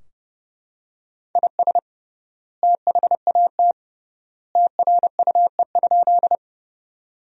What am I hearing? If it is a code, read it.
Morse code, heard as a keyed tone, about 29 wpm, 700 Hz. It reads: IS THAT TRUE?
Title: 3x Word-Spacing - Sentences from Top 400 Words - 30wpm